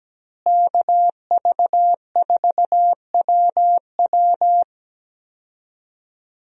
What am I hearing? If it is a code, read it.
Morse code, heard as a keyed tone, about 17 wpm, 700 Hz. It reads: KV4WW
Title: USA Call Signs - 5 Characters - Repeated 2x - 17wpm